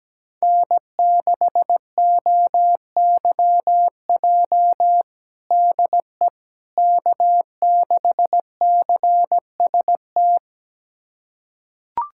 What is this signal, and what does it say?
Morse code, heard as a keyed tone, about 17 wpm, 700 Hz. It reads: N6OYJ DE K6CST
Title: USA Callsign DE Callsign - 5 Characters - 17wpm